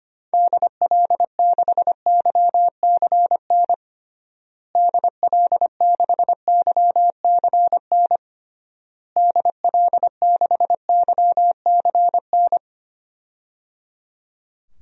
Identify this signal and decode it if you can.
Morse code, heard as a keyed tone, about 25 wpm, 700 Hz. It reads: DL6YCN DL6YCN DL6YCN
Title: DXCC-100 Call Signs - 6 Characters - Repeated 3x - 25wpm